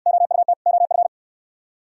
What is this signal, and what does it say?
Morse code, heard as a keyed tone, about 66 wpm, 700 Hz. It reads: OUT OF